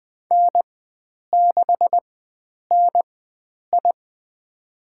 Morse code, heard as a keyed tone, about 20 wpm, spaced as for 9 wpm, 700 Hz. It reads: N6NI
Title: USA Call Signs - 4x Character Spacing - 20wpm